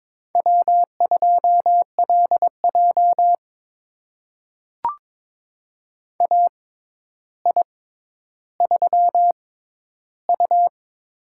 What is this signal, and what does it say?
Morse code, heard as a keyed tone, about 22 wpm, spaced as for 11 wpm, 700 Hz. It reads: W2LJ AI3U
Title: USA Call Signs - 4 Characters - 6x Character Spacing - 22wpm